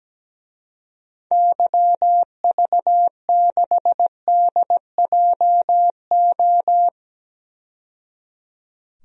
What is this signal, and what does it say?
Morse code, heard as a keyed tone, about 17 wpm, 700 Hz. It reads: YV6DJO
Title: DXCC-100 Call Signs - 6 Characters - 17wpm